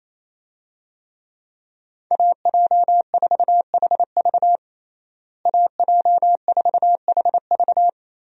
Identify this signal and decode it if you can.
Morse code, heard as a keyed tone, about 28 wpm, 700 Hz. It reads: AJ4HV AJ4HV
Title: USA Call Signs - 5 Characters - Repeated 3x - 28wpm